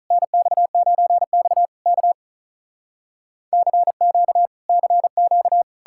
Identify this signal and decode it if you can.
Morse code, heard as a keyed tone, about 46 wpm, 700 Hz. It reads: NX9X K CQ CQ